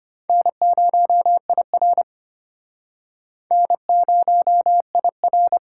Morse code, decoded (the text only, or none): N0IR N0IR